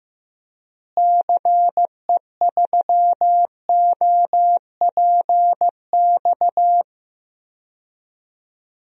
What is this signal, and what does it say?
Morse code, heard as a keyed tone, about 15 wpm, 700 Hz. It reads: CE3OPX
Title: DXCC-100 Call Signs - 6 Characters - Repeated 2x - 15wpm